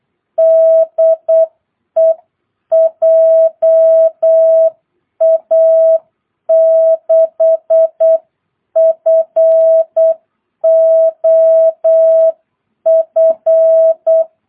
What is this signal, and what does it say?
Morse code, heard as a keyed tone, about 8 wpm, 660 Hz. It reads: DE JA6FOF
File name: jh8cbh / 2022cw